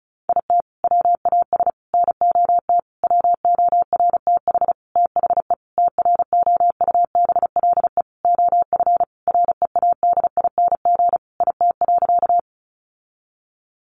Morse code, heard as a keyed tone, about 34 wpm, 700 Hz. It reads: IT WAS NOT WORTH THE TROUBLE OF READING IT.
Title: Sentences from Top 1200 Words 35wpm